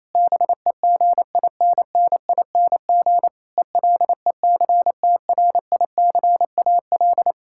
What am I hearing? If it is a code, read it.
Morse code, heard as a keyed tone, about 27 wpm, 700 Hz. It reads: BEGINNING ELECTRICAL
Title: Sentences from Top 1000 Words - Review Part 6 - 28wpm